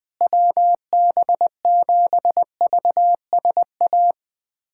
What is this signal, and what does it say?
Morse code, heard as a keyed tone, about 21 wpm, 700 Hz. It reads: WB7VSA
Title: USA Callsign DE Callsign 20wpm